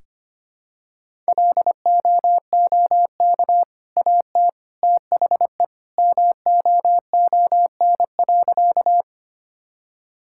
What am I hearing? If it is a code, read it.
Morse code, heard as a keyed tone, about 25 wpm, 700 Hz. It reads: LOOK AT THE MOON.
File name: Easy Sentences with Hints - 25wpm